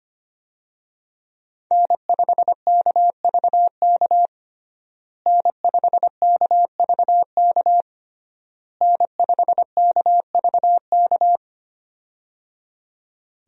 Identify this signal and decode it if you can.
Morse code, heard as a keyed tone, about 25 wpm, 700 Hz. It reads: N5KVK N5KVK N5KVK
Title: USA Call Signs - 5 Characters - Repeated 3x - 25wpm